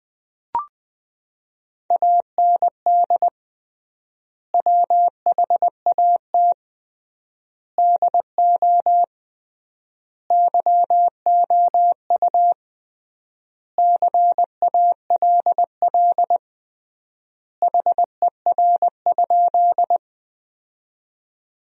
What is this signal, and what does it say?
Morse code, heard as a keyed tone, about 20 wpm, 700 Hz. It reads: AND WHAT DO YOU CALL HER?
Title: 3x Word-Spacing - Sentences from Top 100 Words - 20wpm